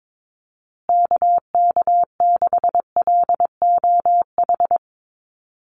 Morse code, heard as a keyed tone, about 22 wpm, 700 Hz. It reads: KK6LOH
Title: USA Call Signs Repeated 2x 22wpm